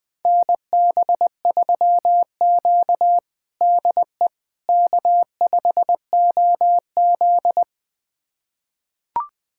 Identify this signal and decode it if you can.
Morse code, heard as a keyed tone, about 21 wpm, 700 Hz. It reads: NB3Q DE K5OZ
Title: USA Callsign DE Callsign - 4 Characters - 20wpm